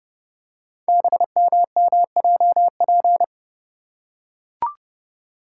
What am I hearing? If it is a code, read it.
Morse code, heard as a keyed tone, about 29 wpm, 700 Hz. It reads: BMMJP
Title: Code Groups - 5 Letter:Numbers - 30wpm